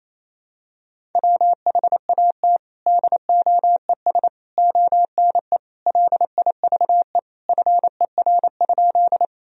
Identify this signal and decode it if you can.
Morse code, heard as a keyed tone, about 28 wpm, 700 Hz. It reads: WHAT DOES ONE LIVE FER?
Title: Sentences from Top 200 Words and Abbreviations 28wpm